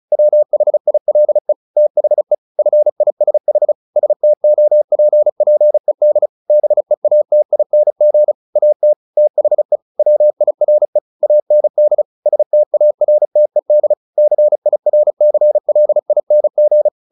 Morse code, heard as a keyed tone, about 34 wpm, 600 Hz. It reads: WHILE THE FISH STOPPED BEATING AT THE WIRE AND STARTED CIRCLING
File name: the old man and the sea 2 35wpm